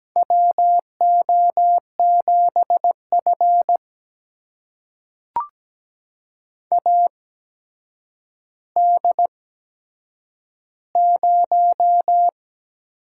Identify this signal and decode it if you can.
Morse code, heard as a keyed tone, about 17 wpm, spaced as for 8 wpm, 700 Hz. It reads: WO7F AD0
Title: USA Call Signs - 4 Characters - 8x Character Spacing - 17wpm